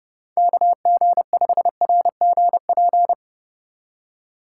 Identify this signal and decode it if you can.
Morse code, heard as a keyed tone, about 37 wpm, 700 Hz. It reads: KG5RGP
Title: Speed Racing - USA 2x3 Call Signs 30wpm